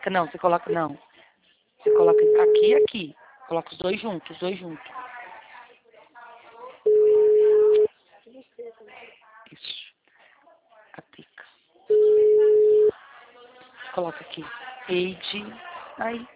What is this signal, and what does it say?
Ringback tone (Europe, 425 Hz), with speech.
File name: out- -2010- - - .29757